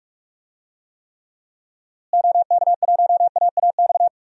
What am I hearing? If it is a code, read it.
Morse code, heard as a keyed tone, about 46 wpm, 700 Hz. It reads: OK1AAX
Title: DXCC-100 Call Signs - 6 Characters - Repeated 2x - 45wpm